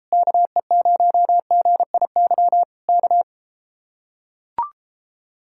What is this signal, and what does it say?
Morse code, heard as a keyed tone, about 33 wpm, 700 Hz. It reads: KE0GIY K